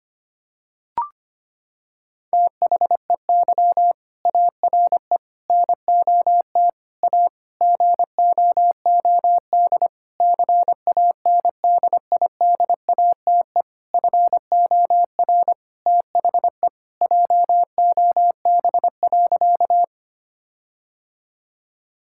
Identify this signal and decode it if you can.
Morse code, heard as a keyed tone, about 25 wpm, 700 Hz. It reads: THEY ARE NOT A GOOD CANDIDATE FOR THE JOB.
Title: Sentences from Top 1600 Words 25wpm